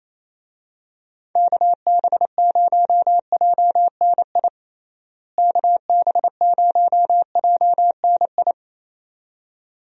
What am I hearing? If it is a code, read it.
Morse code, heard as a keyed tone, about 28 wpm, 700 Hz. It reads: KB0JNI KB0JNI
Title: USA Call Signs - 6 Characters - Repeated 2x - 28wpm